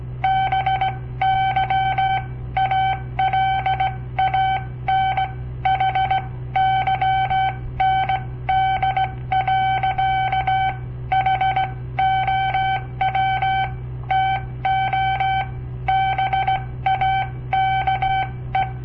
Morse code, heard as a keyed tone, about 17 wpm, spaced as for 15 wpm, 770 Hz. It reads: BY ALAN HYND. HOW TO BAKE
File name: papertapes / local / Code Practice-01